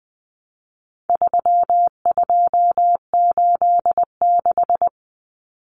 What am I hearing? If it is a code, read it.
Morse code, heard as a keyed tone, about 20 wpm, 700 Hz. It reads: 3286
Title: Common Cut Numbers - 4-Digit 20wpm